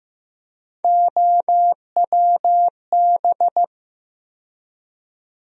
Morse code, heard as a keyed tone, about 15 wpm, 700 Hz. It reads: OWB
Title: USA Call Sign Postfixes - 3 Letters 15wpm